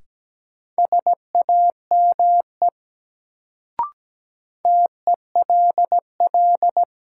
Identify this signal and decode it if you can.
Morse code, heard as a keyed tone, about 17 wpm, 700 Hz. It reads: SAME TELL
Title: Sentences from Top 200 Words - New Word Drill - 17wpm